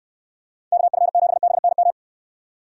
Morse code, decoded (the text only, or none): KF7BNM